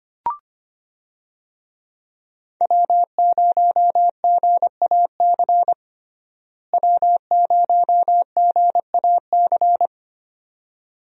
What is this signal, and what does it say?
Morse code, heard as a keyed tone, about 25 wpm, 700 Hz. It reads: W0GAC W0GAC
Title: USA Call Signs - 5 Characters - Repeated 2x - 25wpm